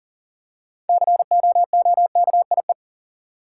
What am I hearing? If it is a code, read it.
Morse code, heard as a keyed tone, about 40 wpm, 700 Hz. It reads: COOKIE
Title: Six Letter Words - 40wpm